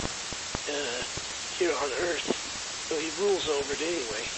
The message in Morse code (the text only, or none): none